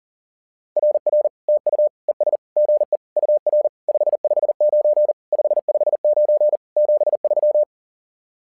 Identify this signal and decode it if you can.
Morse code, heard as a keyed tone, about 40 wpm, 600 Hz. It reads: RR TU ES GE UR 559 559 73
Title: SOTA QSO Practice 40wpm